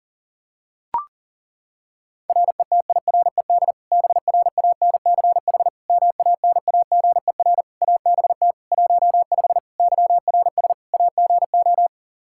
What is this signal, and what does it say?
Morse code, heard as a keyed tone, about 37 wpm, 700 Hz. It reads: RETIRED BRANCH MANAGER ABT 15 YRS AGO
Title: Practice QSO - Exchange 3 - More Info - 40wpm